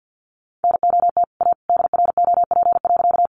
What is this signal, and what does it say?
Morse code, heard as a keyed tone, about 50 wpm, 700 Hz. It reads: NOT A DROP.
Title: Sentences from Top 1400 Words 50wpm